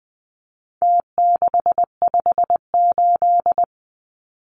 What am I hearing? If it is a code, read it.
Morse code, heard as a keyed tone, about 20 wpm, 700 Hz. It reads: T658
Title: Numbers - Contest Style - 4-Digit 20wpm